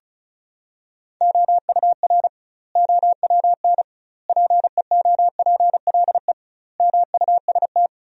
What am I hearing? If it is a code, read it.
Morse code, heard as a keyed tone, about 34 wpm, 700 Hz. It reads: OUR OWN PEOPLE MUST
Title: 2x Word-Spacing - Sentences from Top 300 Words - 35wpm